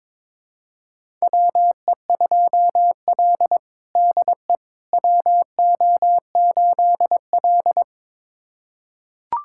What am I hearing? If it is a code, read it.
Morse code, heard as a keyed tone, about 23 wpm, 700 Hz. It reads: WE2L DE WO8L
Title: USA Callsign DE Callsign - 4 Characters - 22wpm